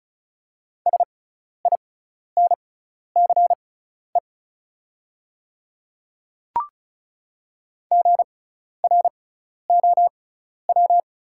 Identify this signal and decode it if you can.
Morse code, heard as a keyed tone, about 36 wpm, spaced as for 12 wpm, 700 Hz. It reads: SINCE GROW
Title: 6x Character Spacing - Top 300 Words - 35wpm